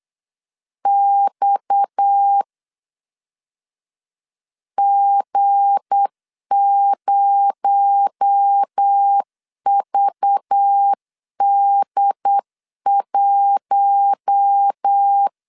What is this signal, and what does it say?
Morse code, heard as a keyed tone, about 8 wpm, 800 Hz. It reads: X G0VD1
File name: LEZIONI / 7 GF9Z / 7 G 8wpm 3118